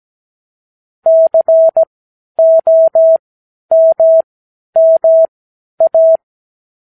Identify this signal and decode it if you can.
Morse code, heard as a keyed tone, about 17 wpm, spaced as for 10 wpm, 650 Hz. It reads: COMMA
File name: k7qo / Lesson 58 Comma